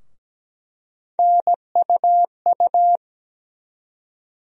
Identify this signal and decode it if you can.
Morse code, heard as a keyed tone, about 17 wpm, 700 Hz. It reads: NUU
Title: USA Call Sign Postfixes - 3 Letters 17wpm